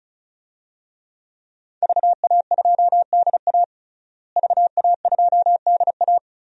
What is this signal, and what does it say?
Morse code, heard as a keyed tone, about 36 wpm, 700 Hz. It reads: VA2DA VA2DA
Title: DXCC-100 Call Signs - 5 Characters - Repeated 3x - 35wpm